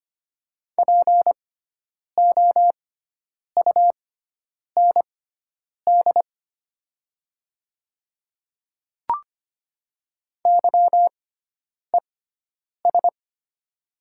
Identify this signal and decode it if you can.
Morse code, heard as a keyed tone, about 25 wpm, spaced as for 9 wpm, 700 Hz. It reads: POUND YES
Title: 6x Character Spacing - Top 500 Words - 25wpm